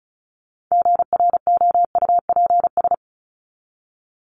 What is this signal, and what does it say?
Morse code, heard as a keyed tone, about 34 wpm, 700 Hz. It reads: GROUPS
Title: ES:S Suffix - Top 100 Words with Base Words 35wpm